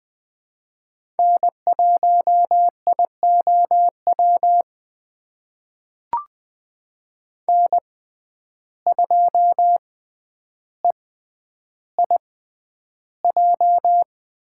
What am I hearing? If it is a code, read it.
Morse code, heard as a keyed tone, about 21 wpm, spaced as for 10 wpm, 700 Hz. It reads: N1IOW N2EIJ
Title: USA Call Signs - 5 Characters - 6x Character Spacing - 20wpm